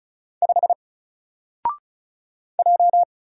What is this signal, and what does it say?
Morse code, heard as a keyed tone, about 36 wpm, 700 Hz. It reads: 5 J
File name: Single-Letter-Number-35wpm